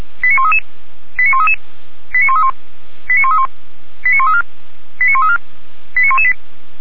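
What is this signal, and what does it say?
Five-tone selective call (ZVEI), heard as quick runs of five tones, tones of about 70 ms. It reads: 89133 89133 89131 89131 89134 89134 89118
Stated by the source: ZVEI five-tone selective call received on an SDR, logged by Feuerwehr Orlamünde (fire brigade).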